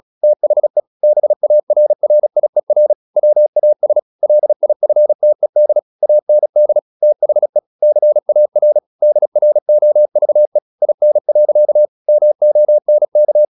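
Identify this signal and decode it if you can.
Morse code, heard as a keyed tone, about 36 wpm, 600 Hz. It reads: THE BARRIER WAS LIFTED AND THE CAR DROVE IN. MONK
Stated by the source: Morse practice at 36 wpm, text of Ken Follett's novel Never, from CW med Gnister.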